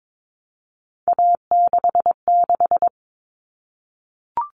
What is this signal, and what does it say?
Morse code, heard as a keyed tone, about 22 wpm, 700 Hz. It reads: A66
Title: All Cut Numbers - 3-Digit 22wpm